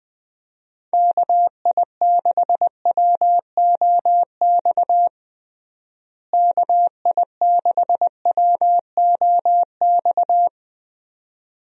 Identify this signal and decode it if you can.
Morse code, heard as a keyed tone, about 20 wpm, 700 Hz. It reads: KI6WOX KI6WOX
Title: USA Call Signs - 6 Characters - Repeated 2x - 20wpm